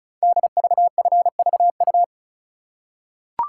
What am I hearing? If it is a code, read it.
Morse code, heard as a keyed tone, about 36 wpm, 700 Hz. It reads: DVFVU
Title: Code Groups - 5 Letters - 35wpm